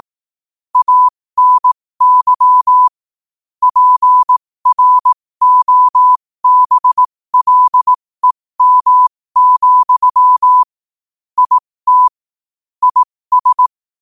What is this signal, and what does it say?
Morse code, heard as a keyed tone, about 18 wpm, spaced as for 15 wpm, 1000 Hz. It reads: ANY PROBLEM, IT IS
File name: qotdfiles / 15wpm